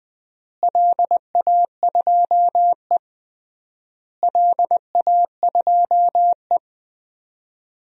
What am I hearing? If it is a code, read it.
Morse code, heard as a keyed tone, about 20 wpm, 700 Hz. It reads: LA2E LA2E
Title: DXCC-100 Call Signs - 4 Characters - Repeated 3x - 20wpm